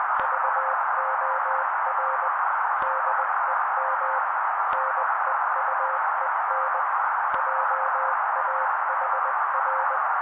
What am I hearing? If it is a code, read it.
Morse code, heard as a keyed tone, about 20 wpm, 530 Hz. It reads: VOR DEM NEUEN JAHR
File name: daily / 18wpm noise